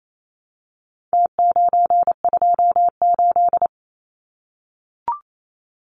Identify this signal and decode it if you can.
Morse code, heard as a keyed tone, about 28 wpm, 700 Hz. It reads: T928
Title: Numbers - Contest Style - 4-Digit 28wpm